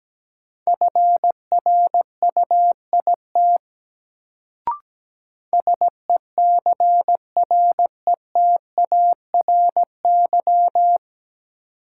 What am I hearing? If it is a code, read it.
Morse code, heard as a keyed tone, about 17 wpm, 700 Hz. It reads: FRUIT SECRETARY
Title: Sentences from Top 1300 Words - New Word Drill - 17wpm